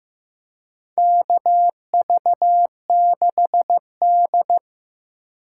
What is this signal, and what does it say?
Morse code, heard as a keyed tone, about 15 wpm, 700 Hz. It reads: KV6D
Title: USA Call Signs - 3x Character Spacing - 15wpm